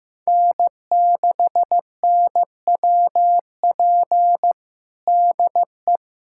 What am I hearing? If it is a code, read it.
Morse code, heard as a keyed tone, about 16 wpm, 700 Hz. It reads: N6NWP DE
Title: USA Callsign DE Callsign - 5 Characters - 15wpm